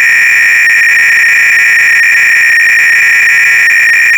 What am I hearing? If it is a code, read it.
300-baud modem data.